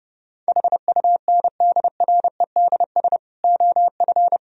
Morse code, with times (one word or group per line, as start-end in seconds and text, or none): 0.5-3.2 HUNDREDS
3.4-4.4 OF